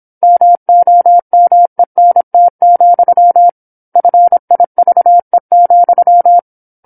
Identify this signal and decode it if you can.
Morse code, heard as a keyed tone, about 27 wpm, 700 Hz. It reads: MOMENT, FIVE,
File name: 08-Chapter-08 26 WPM